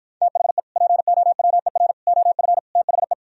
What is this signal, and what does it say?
Morse code, heard as a keyed tone, about 52 wpm, 700 Hz. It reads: THE POWER OF THE